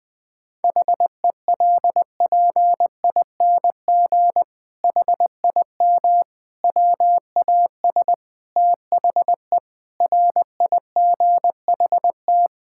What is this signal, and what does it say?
Morse code, heard as a keyed tone, about 19 wpm, 700 Hz. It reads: HELPING HIM WAS THE RIGHT
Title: Sentences with 7-Word Phrases - 20wpm